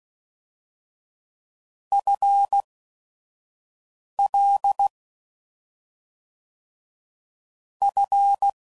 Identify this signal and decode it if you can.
Morse code, heard as a keyed tone, about 16 wpm, spaced as for 5 wpm, 800 Hz. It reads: FL F
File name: Prefixes / FLs & THs / FL Prefixes 1